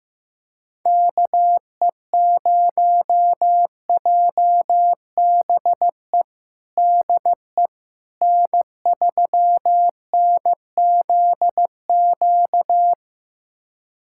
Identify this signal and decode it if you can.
Morse code, heard as a keyed tone, about 15 wpm, 700 Hz. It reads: KE0JBE DE N3NZQ
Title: USA Callsign DE Callsign 15wpm